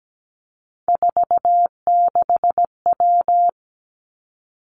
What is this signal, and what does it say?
Morse code, heard as a keyed tone, about 17 wpm, 700 Hz. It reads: 46W
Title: Three Challenging Letter-Number Combinations 17wpm